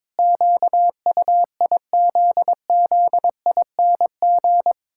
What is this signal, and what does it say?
Morse code, heard as a keyed tone, about 22 wpm, 700 Hz. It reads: QUIZZING
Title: Pangram Challenge v2 - 22wpm